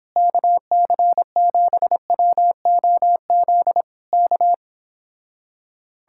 Morse code, heard as a keyed tone, about 26 wpm, 700 Hz. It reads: KC7WOZ K